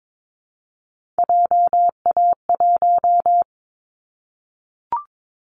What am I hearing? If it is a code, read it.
Morse code, heard as a keyed tone, about 22 wpm, 700 Hz. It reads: JA1
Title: Callsign Prefixes for Top 50 DXCC - 22wpm